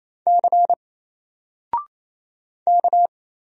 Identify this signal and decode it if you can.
Morse code, heard as a keyed tone, about 27 wpm, 700 Hz. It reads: C K
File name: Single-Letter-Number-28wpm